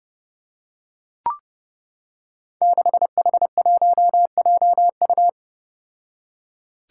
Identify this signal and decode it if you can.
Morse code, heard as a keyed tone, about 31 wpm, 700 Hz. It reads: 6H1JU
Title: Five Challenging Letter-Number Combinations 30wpm